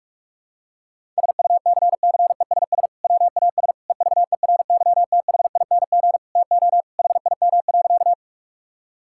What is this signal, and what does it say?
Morse code, heard as a keyed tone, about 41 wpm, 700 Hz. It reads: SUCCESS WAS EVERYTHING TO HIM.
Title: Sentences from Top 1000 Words - Review Part 6 - 45wpm